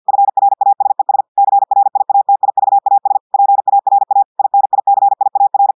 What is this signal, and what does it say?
Morse code, heard as a keyed tone, about 50 wpm, 800 Hz. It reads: FRASES CRIATIVAS PARA INICIAR
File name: portugais texte en clair 2 vitesse 50 mots minutes